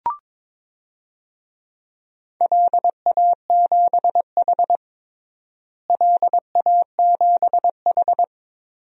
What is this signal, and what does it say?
Morse code, heard as a keyed tone, about 22 wpm, 700 Hz. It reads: LA7H LA7H
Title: DXCC-100 Call Signs - 4 Characters - Repeated 3x - 22wpm